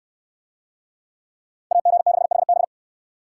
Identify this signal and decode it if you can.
Morse code, heard as a keyed tone, about 63 wpm, 690 Hz. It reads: AG6HB